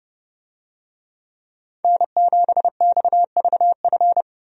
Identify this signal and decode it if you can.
Morse code, heard as a keyed tone, about 30 wpm, 700 Hz. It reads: N7XVF
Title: USA Call Signs - 5 Characters - Repeated 2x - 30wpm